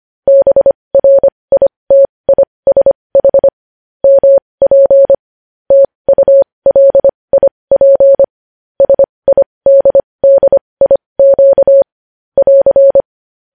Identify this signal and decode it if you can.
Morse code, heard as a keyed tone, about 25 wpm, spaced as for 20 wpm, 550 Hz. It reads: BRITISH MP TULIP SIDDIQ <AR>
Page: news-headlines-in-morse-code-at-20-wpm